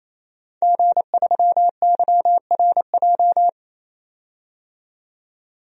Morse code, heard as a keyed tone, about 28 wpm, 700 Hz. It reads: G3YRJ
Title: DXCC-100 Call Signs - 5 Characters - 28wpm